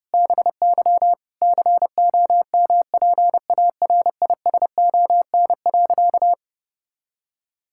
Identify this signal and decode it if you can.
Morse code, heard as a keyed tone, about 30 wpm, 700 Hz. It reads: BY COMPARISON.
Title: Sentences from Top 1600 Words 30wpm